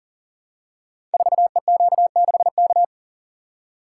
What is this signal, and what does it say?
Morse code, heard as a keyed tone, about 41 wpm, 690 Hz. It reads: 4EQ6K
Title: Code Groups - 5 Letter:Numbers - 40wpm